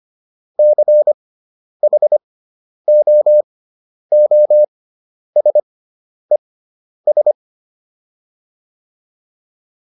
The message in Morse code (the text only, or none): CHOOSES